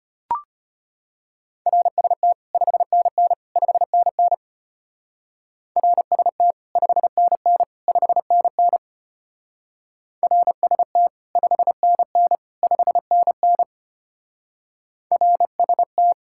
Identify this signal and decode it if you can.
Morse code, heard as a keyed tone, about 32 wpm, 700 Hz. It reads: RST 5NN 5NN RST 5NN 5NN RST 5NN 5NN RST